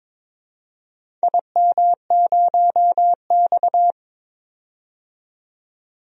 Morse code, heard as a keyed tone, about 22 wpm, 700 Hz. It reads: IM0X
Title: DXCC-100 Call Signs - 4 Characters - 22wpm